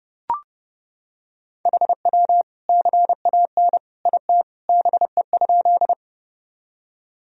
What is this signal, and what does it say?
Morse code, heard as a keyed tone, about 31 wpm, 700 Hz. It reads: HW CAN IT BE?
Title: Sentences from Top 100 Words and Abbreviations 30wpm